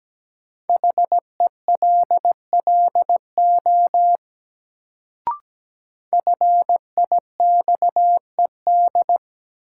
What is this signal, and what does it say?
Morse code, heard as a keyed tone, about 17 wpm, 700 Hz. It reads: HELLO FIXED
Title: Sentences from Top 1100 Words - New Word Drill Alt - 17wpm